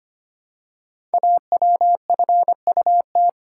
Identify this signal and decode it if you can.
Morse code, heard as a keyed tone, about 25 wpm, 700 Hz. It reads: AWFUT
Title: Code Groups - 5 Letters - 25wpm